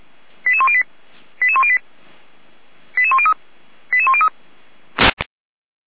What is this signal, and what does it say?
Five-tone selective call (ZVEI), heard as quick runs of five tones, tones of about 70 ms. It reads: 88108 88108 88103 88103